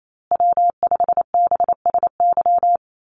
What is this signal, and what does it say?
Morse code, heard as a keyed tone, about 28 wpm, 700 Hz. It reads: W5BSY
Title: USA Call Signs Repeated 2x 28wpm